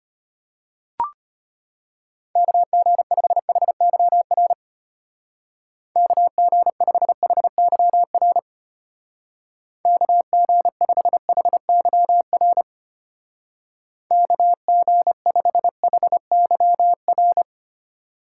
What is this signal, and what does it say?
Morse code, heard as a keyed tone, about 31 wpm, 700 Hz. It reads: KG5HYR KG5HYR KG5HYR KG5HYR